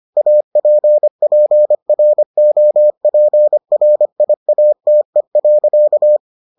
Morse code, heard as a keyed tone, about 25 wpm, 600 Hz. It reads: APPROPRIATE.